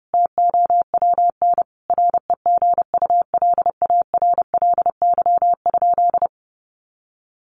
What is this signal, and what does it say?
Morse code, heard as a keyed tone, about 30 wpm, 700 Hz. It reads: TOWN REGULARLY?
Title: Sentences from Top 1400 Words 30wpm